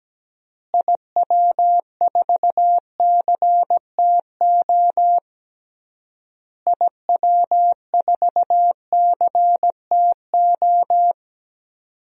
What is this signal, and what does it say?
Morse code, heard as a keyed tone, about 17 wpm, 700 Hz. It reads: IW4CTO IW4CTO
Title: DXCC-100 Call Signs - 6 Characters - Repeated 3x - 17wpm